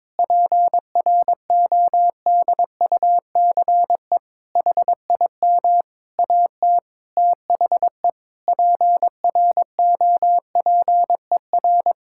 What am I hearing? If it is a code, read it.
Morse code, heard as a keyed tone, about 22 wpm, 700 Hz. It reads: PRODUCE HIM AT THE PROPER
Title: Sentences from Top 1000 Words - Review Part 5 - 22wpm